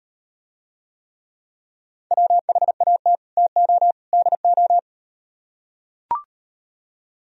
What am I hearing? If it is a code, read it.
Morse code, heard as a keyed tone, about 38 wpm, 700 Hz. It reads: WHAT TO DO